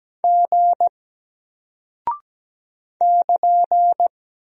Morse code, heard as a keyed tone, about 17 wpm, 700 Hz. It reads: G <KN>
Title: Single Character (Letter:Number:Punctuation:Prosign) Alt 17wpm